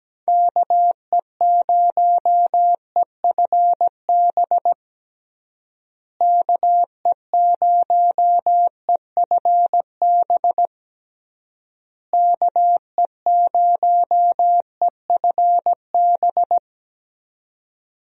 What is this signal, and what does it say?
Morse code, heard as a keyed tone, about 17 wpm, 700 Hz. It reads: KE0EFB KE0EFB KE0EFB
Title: USA Call Signs - 6 Characters - Repeated 3x - 17wpm